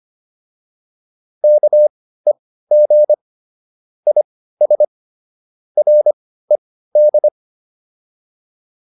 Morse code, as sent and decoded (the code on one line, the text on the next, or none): -.- . --. / .. ... / .-. -..
KEG IS RED